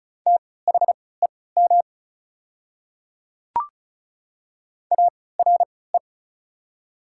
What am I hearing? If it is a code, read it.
Morse code, heard as a keyed tone, about 34 wpm, spaced as for 20 wpm, 690 Hz. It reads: THEM ARE
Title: 3x Character Spacing - Top 100 Words - 35wpm